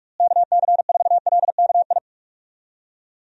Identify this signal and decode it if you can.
Morse code, heard as a keyed tone, about 46 wpm, 700 Hz. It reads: KC4LKI
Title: USA Call Signs - 6 Characters - Repeated 3x - 45wpm